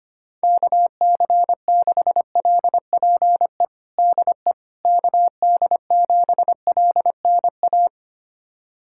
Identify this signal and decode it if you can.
Morse code, heard as a keyed tone, about 25 wpm, 700 Hz. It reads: KC6LPE DE KD7LNA
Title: USA Callsign DE Callsign - 6 Characters - 25wpm